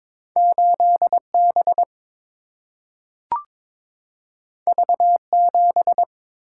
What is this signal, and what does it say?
Morse code, heard as a keyed tone, about 22 wpm, 700 Hz. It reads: 8B V7
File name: practice / 22wpm / Two Letter-Number Combinations - 22wpm